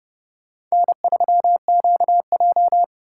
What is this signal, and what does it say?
Morse code, heard as a keyed tone, about 36 wpm, 700 Hz. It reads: N3QJ
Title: Speed Racing - USA 1x2 Call Signs 30wpm